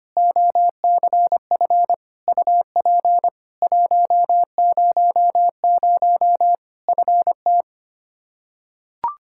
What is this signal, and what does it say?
Morse code, heard as a keyed tone, about 25 wpm, 700 Hz. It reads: OCF UP 100 FT
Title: Practice QSO - Exchange 2 - Info - 25wpm